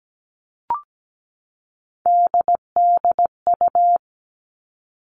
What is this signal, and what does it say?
Morse code, heard as a keyed tone, about 17 wpm, 700 Hz. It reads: DDU
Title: All Cut Numbers - 3-Digit 17wpm